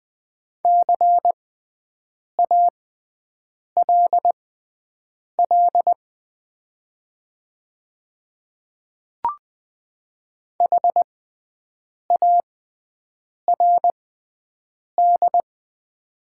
Morse code, heard as a keyed tone, about 20 wpm, spaced as for 7 wpm, 690 Hz. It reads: CALL HARD
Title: 6x Character Spacing - Top 300 Words - 20wpm